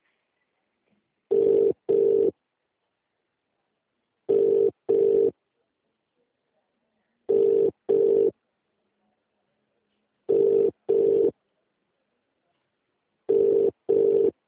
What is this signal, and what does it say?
Ringback tone (UK), 400 + 450 Hz.